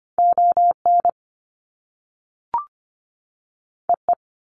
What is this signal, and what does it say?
Morse code, heard as a keyed tone, about 25 wpm, 700 Hz. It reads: ON EE
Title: Top Two Letter Combinations in Words 25wpm